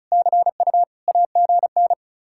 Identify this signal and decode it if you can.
Morse code, heard as a keyed tone, about 43 wpm, 700 Hz. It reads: CU AGN
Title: Speed-Racing Common QSO Phrases Encore - 35wpm